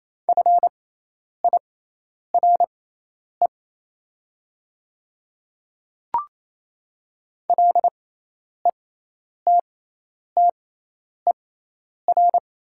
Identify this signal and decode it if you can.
Morse code, heard as a keyed tone, about 28 wpm, spaced as for 10 wpm, 700 Hz. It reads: FIRE LETTER
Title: 6x Character Spacing - Top 500 Words - 28wpm